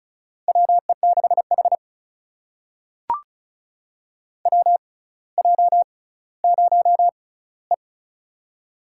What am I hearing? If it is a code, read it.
Morse code, heard as a keyed tone, about 36 wpm, spaced as for 18 wpm, 700 Hz. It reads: WE6H WJ0E
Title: USA Call Signs - 4 Characters - 6x Character Spacing - 35wpm